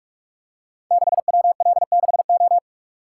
Morse code, heard as a keyed tone, about 44 wpm, 700 Hz. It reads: 6WL6O
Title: Code Groups - 5 Letter:Numbers - 45wpm